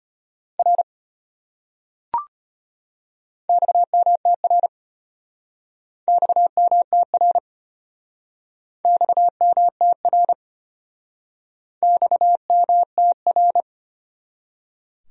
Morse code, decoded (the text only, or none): R XMTR XMTR XMTR XMTR